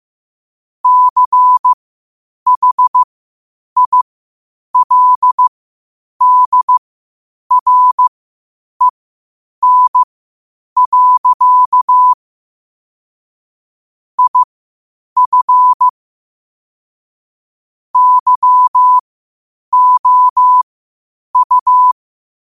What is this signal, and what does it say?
Morse code, heard as a keyed tone, about 15 wpm, spaced as for 8 wpm, 1000 Hz. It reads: CHILDREN. IF YOU
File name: qotdfiles / 8wpm